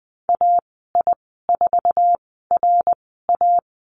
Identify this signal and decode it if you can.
Morse code, heard as a keyed tone, about 20 wpm, spaced as for 15 wpm, 700 Hz. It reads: AI4RA
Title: 055 Lesson 10g - Letter r - Call Signs 20wpm